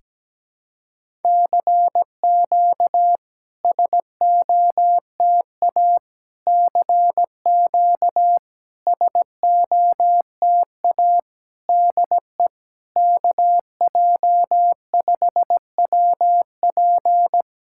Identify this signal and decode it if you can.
Morse code, heard as a keyed tone, about 17 wpm, 700 Hz. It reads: CQ SOTA CQ SOTA DE KJ5WP
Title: SOTA QSO Practice 17wpm